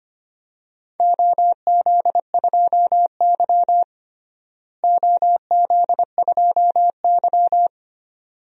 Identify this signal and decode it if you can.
Morse code, heard as a keyed tone, about 25 wpm, 700 Hz. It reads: OZ2Y OZ2Y
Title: DXCC-100 Call Signs - 4 Characters - Repeated 2x - 25wpm